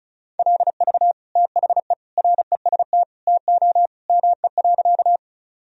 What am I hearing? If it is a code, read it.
Morse code, heard as a keyed tone, about 36 wpm, 700 Hz. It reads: LV THE REST TO ME.
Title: Sentences from Top 500 Words and Abbreviations 35wpm